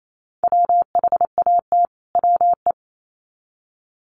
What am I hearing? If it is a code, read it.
Morse code, heard as a keyed tone, about 28 wpm, 700 Hz. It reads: WHAT WE
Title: Sets of 2 Words 28wpm